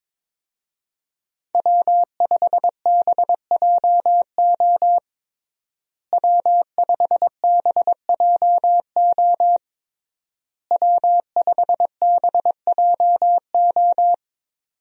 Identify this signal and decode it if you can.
Morse code, heard as a keyed tone, about 22 wpm, 700 Hz. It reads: W5BJO W5BJO W5BJO
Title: USA Call Signs - 5 Characters - Repeated 3x - 22wpm